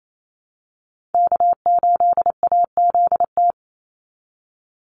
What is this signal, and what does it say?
Morse code, heard as a keyed tone, about 28 wpm, 700 Hz. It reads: K8AZT
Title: Call Signs Repeated 3x - 28wpm